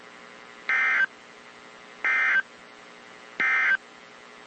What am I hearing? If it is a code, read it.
Emergency Alert System (SAME) tones, heard as harsh buzzing bursts. It is heard from 0.6 to 3.9 s.
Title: WZ2523 - NOAA Weather Radio